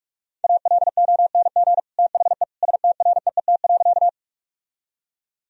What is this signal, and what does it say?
Morse code, heard as a keyed tone, about 43 wpm, 700 Hz. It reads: ALONG THE STREET.
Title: Sentences from Top 1000 Words - Review Part 3 - 45wpm